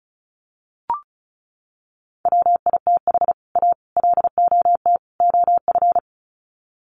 Morse code, heard as a keyed tone, about 34 wpm, 700 Hz. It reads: WITH A LOT OF
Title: Sets of 4 Words 35wpm